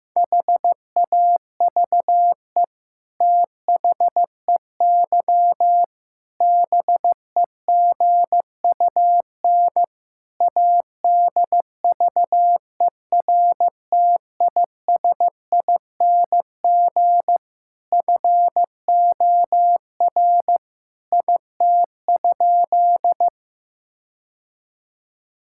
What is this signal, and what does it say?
Morse code, heard as a keyed tone, about 15 wpm, 700 Hz. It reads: HAVE THEY BEGUN ADVERTISING FOR IT?
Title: Sentences from Top 1000 Words - Review Part 6 - 15wpm